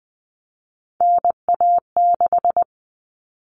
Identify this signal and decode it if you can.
Morse code, heard as a keyed tone, about 20 wpm, 700 Hz. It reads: NA6
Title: Common Cut Numbers - 3-Digit 20wpm